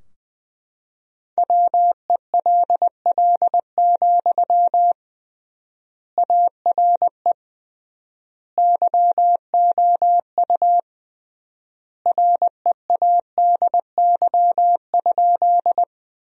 Morse code, heard as a keyed tone, about 20 wpm, 700 Hz. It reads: WELL, ARE YOU READY?
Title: 3x Word-Spacing - Sentences from Top 400 Words - 20wpm